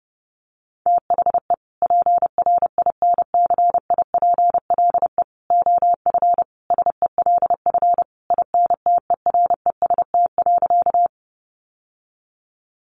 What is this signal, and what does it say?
Morse code, heard as a keyed tone, about 30 wpm, 700 Hz. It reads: THE PRINCIPLE OF SELF INTEREST.
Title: Sentences from Top 1400 Words 30wpm